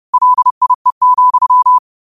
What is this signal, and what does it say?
Morse code, heard as a keyed tone, about 30 wpm, 1000 Hz. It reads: LIE,
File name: qotdfiles / 30wpm